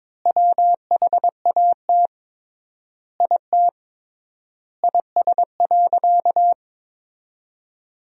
Morse code, heard as a keyed tone, about 22 wpm, 700 Hz. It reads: WHAT IT IS.
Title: 3x Word-Spacing - Sentences from Top 200 Words - 22wpm